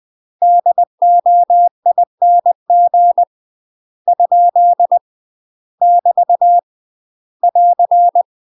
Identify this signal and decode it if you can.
Morse code, heard as a keyed tone, about 20 wpm, 700 Hz. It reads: DOING ? <BT> <AR>
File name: UK MASTER QSOs TRAINING / 20WPM / UK MASTER QSOs 0009